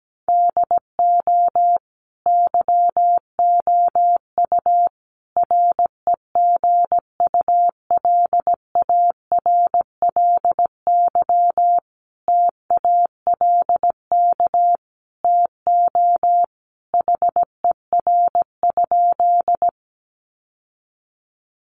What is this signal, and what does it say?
Morse code, heard as a keyed tone, about 17 wpm, 700 Hz. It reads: DO YOU REGULARLY TALK TO HER?
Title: Sentences from Top 1400 Words 17wpm